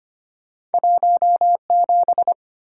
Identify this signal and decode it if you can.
Morse code, heard as a keyed tone, about 25 wpm, 700 Hz. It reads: 17
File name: Two Letter-Number Combinations - 25wpm